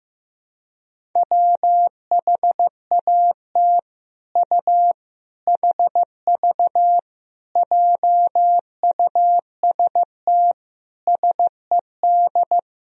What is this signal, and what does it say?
Morse code, heard as a keyed tone, about 14 wpm, 700 Hz. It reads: WHAT U HV JUST SED
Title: Sentences from Top 200 Words and Abbreviations 15wpm